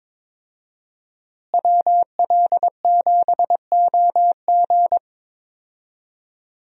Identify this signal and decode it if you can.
Morse code, heard as a keyed tone, about 22 wpm, 700 Hz. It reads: WL7OG
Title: USA Call Signs - 5 Characters - Repeated 2x - 22wpm